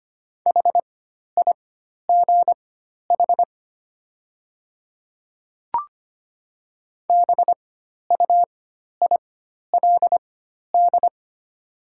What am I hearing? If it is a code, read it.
Morse code, heard as a keyed tone, about 25 wpm, spaced as for 12 wpm, 690 Hz. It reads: HIGH BUILD
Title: 4x Character Spacing - Top 300 Words - 25wpm